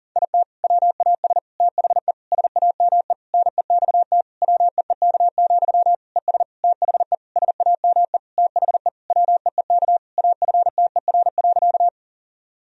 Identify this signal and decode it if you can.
Morse code, heard as a keyed tone, about 40 wpm, 700 Hz. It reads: IT WAS THE SAME NEXT WEEK, ES THE SAME THE WEEK AFTER.
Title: Sentences from Top 500 Words and Abbreviations 40wpm